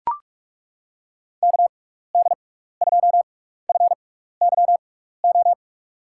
Morse code, heard as a keyed tone, about 42 wpm, spaced as for 15 wpm, 700 Hz. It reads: KD2FYO
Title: USA Call Signs - 6 Characters - 6x Character Spacing - 45wpm